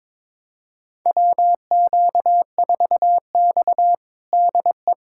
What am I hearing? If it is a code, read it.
Morse code, heard as a keyed tone, about 23 wpm, 700 Hz. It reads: WQ4X DE
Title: USA Callsign DE Callsign - 4 Characters - 22wpm